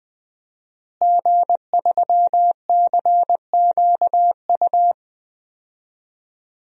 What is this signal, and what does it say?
Morse code, heard as a keyed tone, about 20 wpm, 700 Hz. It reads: G3CQU
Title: DXCC-100 Call Signs - 5 Characters - Repeated 2x - 20wpm